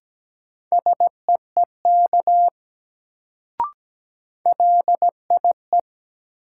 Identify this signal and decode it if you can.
Morse code, heard as a keyed tone, about 17 wpm, 700 Hz. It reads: SEEK LIE
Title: Sentences from Top 1600 Words - New Word Drill - 17wpm